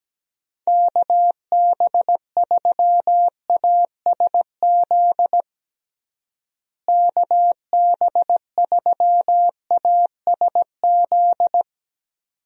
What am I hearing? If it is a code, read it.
Morse code, heard as a keyed tone, about 17 wpm, 700 Hz. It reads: KB3ASZ KB3ASZ
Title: USA Call Signs - 6 Characters - Repeated 2x - 17wpm